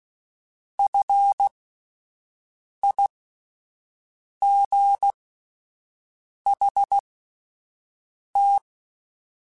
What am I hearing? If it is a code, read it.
Morse code, heard as a keyed tone, about 16 wpm, spaced as for 5 wpm, 800 Hz. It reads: FIGHT
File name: Alphabetical / Alphabetical mp3s / Fs @5